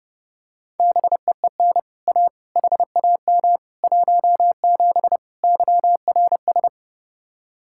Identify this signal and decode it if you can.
Morse code, heard as a keyed tone, about 32 wpm, 700 Hz. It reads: BEEN A HAM 17 YRS